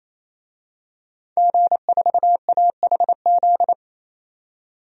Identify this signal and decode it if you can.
Morse code, heard as a keyed tone, about 27 wpm, 700 Hz. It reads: G4AHZ
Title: DXCC-100 Call Signs - 5 Characters - 28wpm